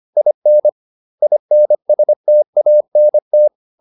Morse code, heard as a keyed tone, about 25 wpm, 600 Hz. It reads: IN INSTANT